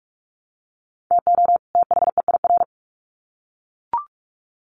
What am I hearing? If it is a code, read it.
Morse code, heard as a keyed tone, about 44 wpm, 700 Hz. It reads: TO THEIR
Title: Sets of 2 Words 45wpm